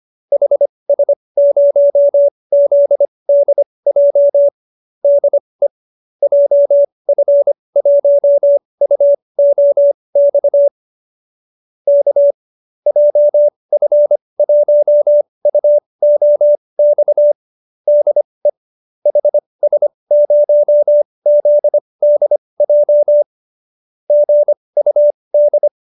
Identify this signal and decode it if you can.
Morse code, heard as a keyed tone, about 25 wpm, spaced as for 20 wpm, 590 Hz. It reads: HS0ZDJ DE JF1UOX K JF1UOX DE HS0ZDJ GUD
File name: qso216 20